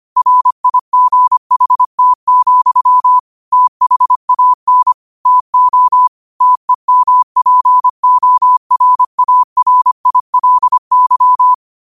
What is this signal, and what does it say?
Morse code, heard as a keyed tone, about 25 wpm, 1000 Hz. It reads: RIGHT, THAN TO TEMPORARILY